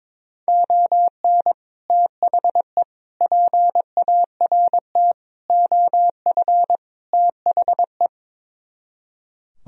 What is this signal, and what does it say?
Morse code, heard as a keyed tone, about 22 wpm, 700 Hz. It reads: ON THE PART OF THE
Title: Speed-Racing - Sets of 5 Words 22wpm